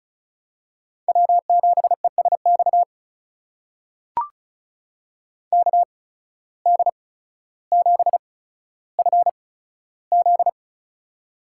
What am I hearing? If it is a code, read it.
Morse code, heard as a keyed tone, about 36 wpm, spaced as for 15 wpm, 700 Hz. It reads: W7ESX KD7FZ
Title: USA Call Signs - 5 Characters - 8x Character Spacing - 35wpm